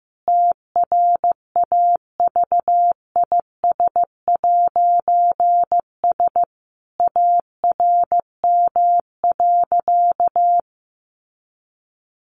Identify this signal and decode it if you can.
Morse code, heard as a keyed tone, about 15 wpm, 700 Hz. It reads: TRAVIS'S ARM.
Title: A Sound Of Thunder 15wpm